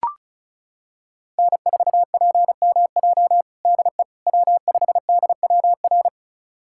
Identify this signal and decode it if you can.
Morse code, heard as a keyed tone, about 36 wpm, 700 Hz. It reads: N4PMJ DE W5DWR
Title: USA Callsign DE Callsign - 5 Characters - 35wpm